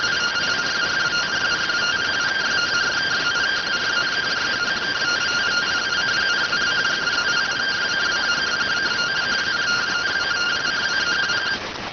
Two-tone teleprinter signal, about 170 Hz shift.